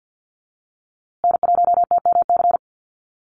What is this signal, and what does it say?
Morse code, heard as a keyed tone, about 54 wpm, 700 Hz. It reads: N1TMC
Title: USA Call Signs Repeated 2x 50wpm